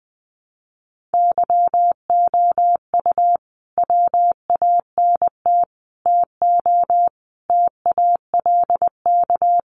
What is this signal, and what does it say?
Morse code, heard as a keyed tone, about 19 wpm, 700 Hz. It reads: YOU WANT TO TALK
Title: Sets of 5 Words Encore 20wpm